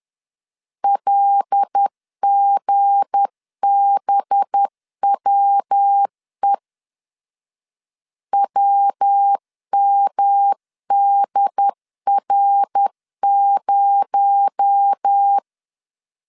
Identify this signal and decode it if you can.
Morse code, heard as a keyed tone, about 10 wpm, 800 Hz. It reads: LGBWE WMDR0
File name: LEZIONI / 7 GF9Z / 7 G 10wpm 2501